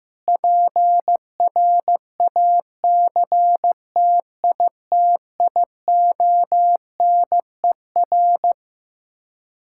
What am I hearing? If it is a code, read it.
Morse code, heard as a keyed tone, about 16 wpm, 700 Hz. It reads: PRACTITIONER
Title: Practice QSO - Exchange 3 - More Info - 15wpm